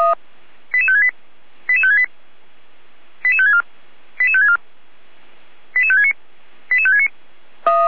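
Five-tone selective call (ZVEI), heard as quick runs of five tones, tones of about 70 ms. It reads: 88568 88568 88564 88564 88569 88569
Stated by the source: ZVEI five-tone call 88355 received on SDR.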